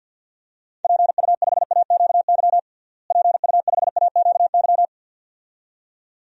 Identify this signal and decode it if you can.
Morse code, heard as a keyed tone, about 50 wpm, 700 Hz. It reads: PU5AQY PU5AQY
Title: DXCC-100 Call Signs - 6 Characters - Repeated 2x - 50wpm